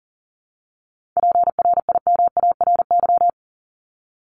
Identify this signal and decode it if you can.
Morse code, heard as a keyed tone, about 39 wpm, 700 Hz. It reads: PRIMARY
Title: LY Suffix - Top 100 Words with Base Words 40wpm